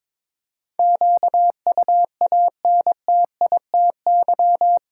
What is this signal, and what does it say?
Morse code, heard as a keyed tone, about 22 wpm, 700 Hz. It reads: QUANTITY
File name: practice / 22wpm / Eight Letter Words - 22wpm